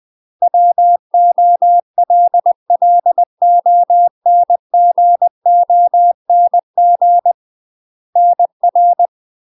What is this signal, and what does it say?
Morse code, heard as a keyed tone, about 20 wpm, 700 Hz. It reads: WOLLONGONG NR
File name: UK MASTER QSOs TRAINING / 20WPM / UK MASTER QSOs 0059